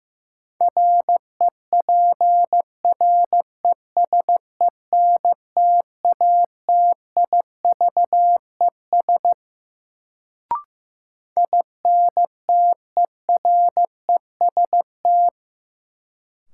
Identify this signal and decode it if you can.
Morse code, heard as a keyed tone, about 15 wpm, 700 Hz. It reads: REPRESENTATIVES INTEREST
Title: Sets of 5 Words Encore - New Word Drill - 15wpm